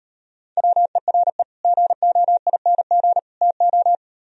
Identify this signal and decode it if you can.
Morse code, heard as a keyed tone, about 38 wpm, 700 Hz. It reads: WERE GOING TO